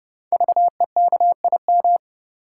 Morse code, heard as a keyed tone, about 28 wpm, 690 Hz. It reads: VEKIM
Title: Code Groups - 5 Letters - 30wpm